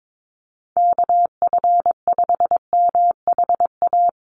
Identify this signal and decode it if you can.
Morse code, heard as a keyed tone, about 22 wpm, 700 Hz. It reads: KF5MHA
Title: USA Call Signs 22wpm